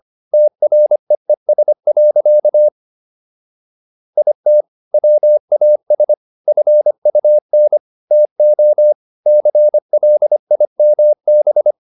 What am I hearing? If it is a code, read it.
Morse code, heard as a keyed tone, about 25 wpm, 600 Hz. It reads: TREES. IT WAS FUN TO CLIMB